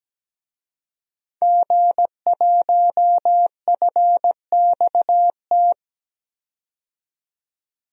Morse code, heard as a keyed tone, about 17 wpm, 700 Hz. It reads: G1FXT
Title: DXCC-100 Call Signs - 5 Characters - 17wpm